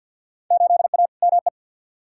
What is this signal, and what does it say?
Morse code, heard as a keyed tone, about 47 wpm, 700 Hz. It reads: 8A ME
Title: Field Day Exchanges - 50wpm